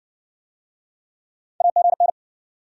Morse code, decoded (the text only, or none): AGN